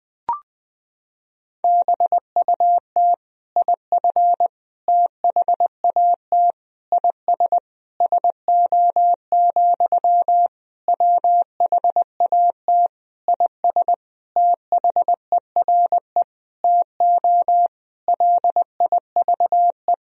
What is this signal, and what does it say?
Morse code, heard as a keyed tone, about 20 wpm, 700 Hz. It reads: BUT IF THAT IS SO, WHAT IS THERE TO LIVE
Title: Sentences from Top 200 Words and Abbreviations 20wpm